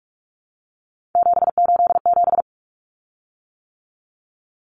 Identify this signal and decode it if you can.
Morse code, heard as a keyed tone, about 44 wpm, 700 Hz. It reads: 787
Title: Common Cut Numbers - 3-Digit 45wpm